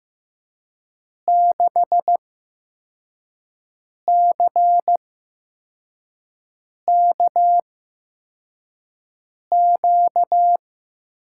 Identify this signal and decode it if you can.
Morse code, heard as a keyed tone, about 15 wpm, spaced as for 4 wpm, 700 Hz. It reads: 6CKQ
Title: USA Call Signs - 6 Characters - 8x Character Spacing - 15wpm